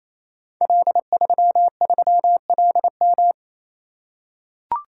Morse code, heard as a keyed tone, about 27 wpm, 700 Hz. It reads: L33LM
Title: Code Groups - 5 Letter:Numbers - 28wpm